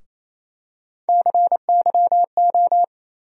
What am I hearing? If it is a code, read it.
Morse code, heard as a keyed tone, about 28 wpm, 700 Hz. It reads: CYO
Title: USA Call Sign Postfixes - 3 Letters 28wpm